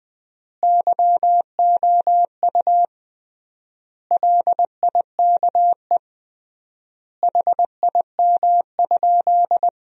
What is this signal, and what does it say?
Morse code, heard as a keyed tone, about 20 wpm, 700 Hz. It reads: YOU LIKE HIM?
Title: 3x Word-Spacing - Sentences from Top 100 Words - 20wpm